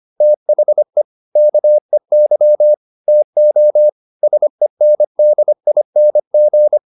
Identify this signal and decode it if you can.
Morse code, heard as a keyed tone, about 25 wpm, 600 Hz. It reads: THE KEY TO SENDING